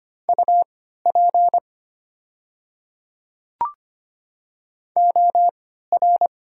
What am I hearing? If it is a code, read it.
Morse code, heard as a keyed tone, about 25 wpm, spaced as for 14 wpm, 700 Hz. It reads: UP OR
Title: 3x Character Spacing - Top 300 Words - 25wpm